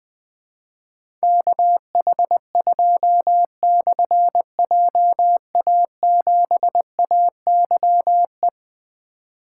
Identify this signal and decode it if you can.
Morse code, heard as a keyed tone, about 20 wpm, 700 Hz. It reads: KH2/JA7AYE
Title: Difficult Call Signs 20wpm